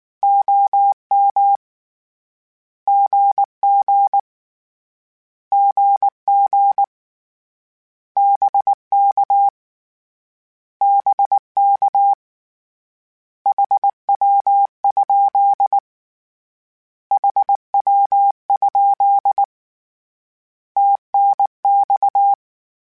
Morse code, heard as a keyed tone, about 19 wpm, 800 Hz. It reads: OM GG GG BK BK HW? HW? TNX
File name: Abreviations / Abreviations mp3s / Abreviations@18